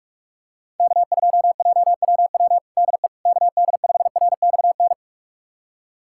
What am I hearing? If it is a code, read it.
Morse code, heard as a keyed tone, about 46 wpm, 700 Hz. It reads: K2JWW DE KD5RXN
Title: USA Callsign DE Callsign 45wpm